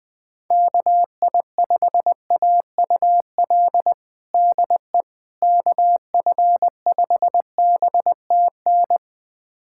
Morse code, heard as a keyed tone, about 20 wpm, 700 Hz. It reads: KI5AUL DE KF5BTN
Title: USA Callsign DE Callsign - 6 Characters - 20wpm